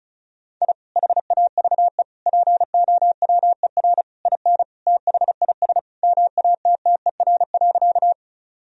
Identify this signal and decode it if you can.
Morse code, heard as a keyed tone, about 36 wpm, 700 Hz. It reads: I HAVE POWER IN THIS MATTER.
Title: Sentences from Top 1000 Words - Review Part 2 - 35wpm